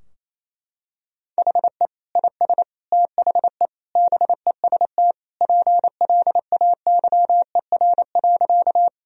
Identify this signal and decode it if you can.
Morse code, heard as a keyed tone, about 27 wpm, 700 Hz. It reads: HE IS THE BEST PLAYER.
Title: Sentences from Top 1000 Words - Review Part 4 - 28wpm